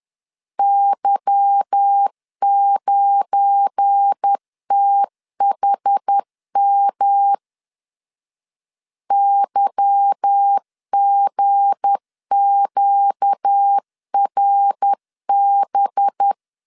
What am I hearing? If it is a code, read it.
Morse code, heard as a keyed tone, about 10 wpm, 800 Hz. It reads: Y9THM YGQRB
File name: LEZIONI / 8 YJ45 / 8 Y 10wpm 2615